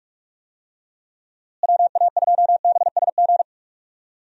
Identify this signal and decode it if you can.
Morse code, heard as a keyed tone, about 46 wpm, 700 Hz. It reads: WA2BSG
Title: USA Call Signs - 6 Characters - Repeated 2x - 45wpm